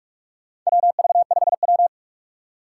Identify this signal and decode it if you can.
Morse code, heard as a keyed tone, about 46 wpm, 700 Hz. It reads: WV5W
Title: USA Call Signs - 3x Character Spacing - 45wpm